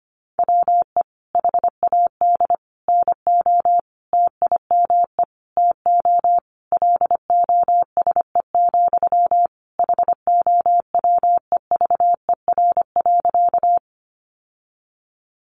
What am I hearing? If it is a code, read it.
Morse code, heard as a keyed tone, about 25 wpm, 690 Hz. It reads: WE HAD NO TIME TO LOSE, HOWEVER.
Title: Sentences from Top 1100 Words 25wpm